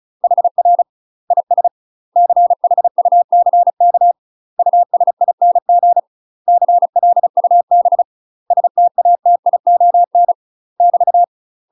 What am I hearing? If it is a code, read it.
Morse code, heard as a keyed tone, about 36 wpm, 700 Hz. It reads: HR IS CHUCK USING CLUB STATION <BT>
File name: UK MASTER QSOs TRAINING / 35WPM / UK MASTER QSOs 0007